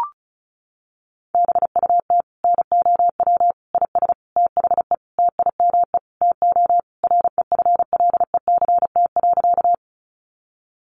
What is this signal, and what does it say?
Morse code, heard as a keyed tone, about 34 wpm, 700 Hz. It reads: BUT NOW IS THE TIME TO REFLECT.
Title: Sentences from Top 1500 Words 35wpm